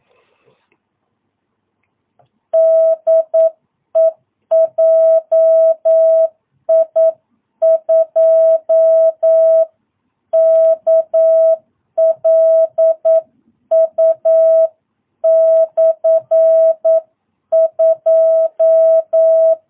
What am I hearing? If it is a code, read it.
Morse code, heard as a keyed tone, about 9 wpm, 660 Hz. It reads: DE JI2KLU /2
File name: jh8cbh / 2022cw